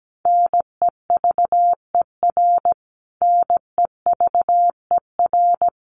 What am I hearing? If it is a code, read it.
Morse code, heard as a keyed tone, about 17 wpm, 700 Hz. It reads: NEVER NEVER
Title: Top 500 Words Repeated 17wpm